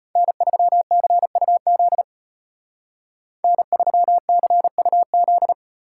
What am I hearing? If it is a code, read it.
Morse code, heard as a keyed tone, about 33 wpm, 700 Hz. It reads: N3CUZ N3CUZ